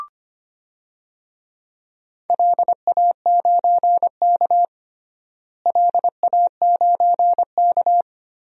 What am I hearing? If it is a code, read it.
Morse code, heard as a keyed tone, about 25 wpm, 700 Hz. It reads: LA9K LA9K
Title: DXCC-100 Call Signs - 4 Characters - Repeated 2x - 25wpm